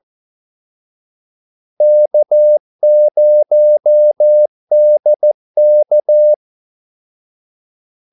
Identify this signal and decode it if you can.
Morse code, heard as a keyed tone, about 14 wpm, 600 Hz. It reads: K0DK
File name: Callsigns-14wpm